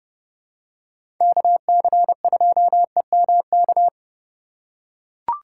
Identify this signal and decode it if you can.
Morse code, heard as a keyed tone, about 31 wpm, 700 Hz. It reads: KC2EMK
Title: USA Call Signs - 6 Characters - 3x Character Spacing - 30wpm